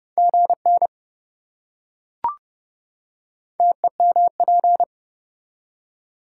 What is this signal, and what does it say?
Morse code, heard as a keyed tone, about 27 wpm, 700 Hz. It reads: GN TEMP